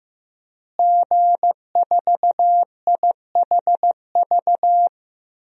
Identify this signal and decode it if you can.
Morse code, heard as a keyed tone, about 15 wpm, 700 Hz. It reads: G4IHV
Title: DXCC-100 Call Signs - 5 Characters - Repeated 2x - 15wpm